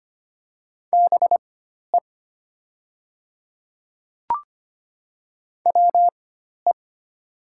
Morse code, heard as a keyed tone, about 25 wpm, spaced as for 12 wpm, 700 Hz. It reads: BE WE
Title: 4x Character Spacing - Top 100 Words - 25wpm